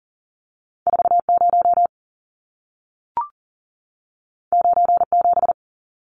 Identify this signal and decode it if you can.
Morse code, heard as a keyed tone, about 40 wpm, 700 Hz. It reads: 40 97
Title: Numbers - 2-Digit 40wpm